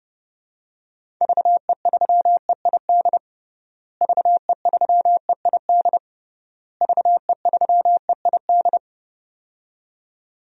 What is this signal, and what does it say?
Morse code, heard as a keyed tone, about 29 wpm, 700 Hz. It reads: VE3EID VE3EID VE3EID
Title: DXCC-100 Call Signs - 6 Characters - Repeated 3x - 30wpm